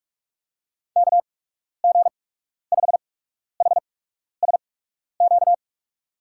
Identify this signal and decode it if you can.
Morse code, heard as a keyed tone, about 44 wpm, spaced as for 12 wpm, 700 Hz. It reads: KG5HSQ
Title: USA Call Signs - 6 Characters - 8x Character Spacing - 45wpm